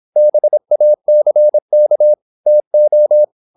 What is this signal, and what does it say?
Morse code, heard as a keyed tone, about 26 wpm, 600 Hz. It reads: BACK TO
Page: Never 26wpm | CW med Gnister